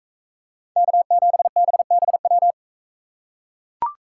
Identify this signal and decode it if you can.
Morse code, heard as a keyed tone, about 38 wpm, 700 Hz. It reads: K7BBW